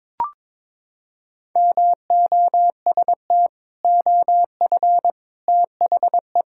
Morse code, heard as a keyed tone, about 22 wpm, 700 Hz. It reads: MOST OF THE
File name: practice / 22wpm / Sentences with 3-Word Phrases - 22wpm